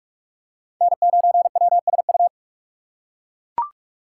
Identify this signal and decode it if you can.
Morse code, heard as a keyed tone, about 44 wpm, 700 Hz. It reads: N9WSU
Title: USA Call Signs - 6x Character Spacing - 45wpm